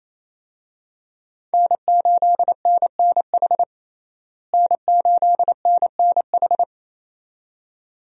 Morse code, heard as a keyed tone, about 28 wpm, 700 Hz. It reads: N8NNH N8NNH
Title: USA Call Signs - 5 Characters - Repeated 2x - 28wpm